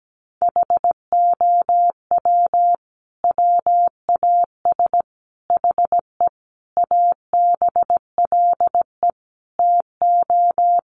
Morse code, heard as a keyed tone, about 17 wpm, 700 Hz. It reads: HOW WAS HE ABLE TO
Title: Sentences from Top 1400 Words 17wpm